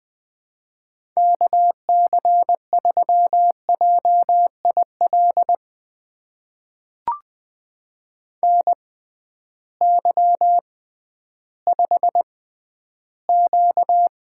Morse code, heard as a keyed tone, about 20 wpm, spaced as for 12 wpm, 700 Hz. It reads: KC3JIL NY5Q
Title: USA Call Signs - 6x Character Spacing - 20wpm